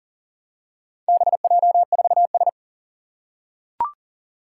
Morse code, heard as a keyed tone, about 40 wpm, 700 Hz. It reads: BJ4S
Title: Four Challenging Letter-Number Combinations 40wpm